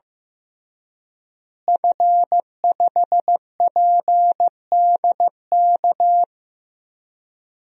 Morse code, heard as a keyed tone, about 15 wpm, 700 Hz. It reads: F5PDK
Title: DXCC-100 Call Signs - 5 Characters - Repeated 3x - 15wpm